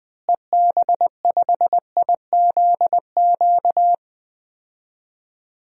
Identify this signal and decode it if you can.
Morse code, heard as a keyed tone, about 20 wpm, 700 Hz. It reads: EB5IZQ
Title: DXCC-100 Call Signs - 6 Characters - Repeated 3x - 20wpm